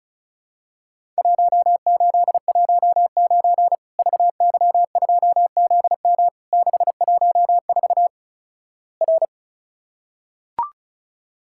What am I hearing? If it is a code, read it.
Morse code, heard as a keyed tone, about 36 wpm, 700 Hz. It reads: 1819 VY2ZM 614 R
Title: Worked All Europe DX Contest - QTC Exchange - 35wpm